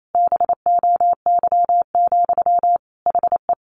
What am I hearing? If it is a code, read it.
Morse code, heard as a keyed tone, about 28 wpm, 700 Hz. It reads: BOY, HE
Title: Sentences from Top 1500 Words 28wpm